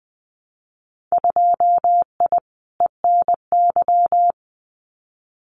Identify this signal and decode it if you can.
Morse code, heard as a keyed tone, about 19 wpm, 700 Hz. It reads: 2I ENY
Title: Winter Field Day Exchanges Encore 20wpm